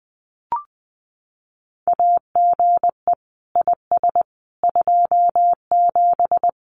Morse code, heard as a keyed tone, about 20 wpm, 700 Hz. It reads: AGE IS 27
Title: Common QSO Phrases Encore 20wpm